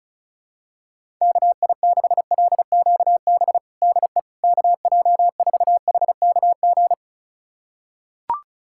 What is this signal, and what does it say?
Morse code, heard as a keyed tone, about 34 wpm, 700 Hz. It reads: KI6LQB DE KJ4HKG
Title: USA Callsign DE Callsign - 6 Characters - 35wpm